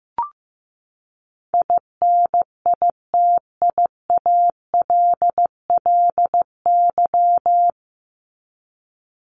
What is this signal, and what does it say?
Morse code, heard as a keyed tone, about 15 wpm, 700 Hz. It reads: INITIALLY
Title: Top 1251-1500 Words 15wpm